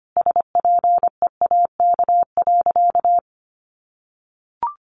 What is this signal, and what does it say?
Morse code, heard as a keyed tone, about 25 wpm, 700 Hz. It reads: SPEAK.
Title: Sentences from Top 1100 Words 25wpm